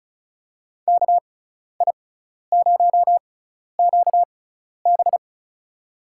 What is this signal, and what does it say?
Morse code, heard as a keyed tone, about 34 wpm, spaced as for 12 wpm, 700 Hz. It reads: KI0QB
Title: USA Call Signs - 5 Characters - 6x Character Spacing - 35wpm